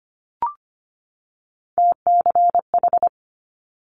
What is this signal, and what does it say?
Morse code, heard as a keyed tone, about 25 wpm, 700 Hz. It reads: TCH
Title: Multi-letter Phonemes 25wpm